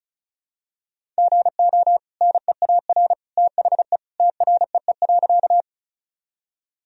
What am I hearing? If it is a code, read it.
Morse code, heard as a keyed tone, about 34 wpm, 700 Hz. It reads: GO NEAR THE TREE.
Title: Sentences from Top 1000 Words - Review Part 1 - 35wpm